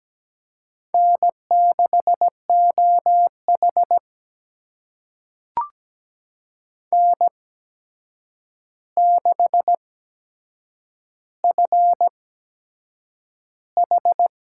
Morse code, heard as a keyed tone, about 17 wpm, spaced as for 7 wpm, 700 Hz. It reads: N6OH N6FH
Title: USA Call Signs - 8x Character Spacing - 17wpm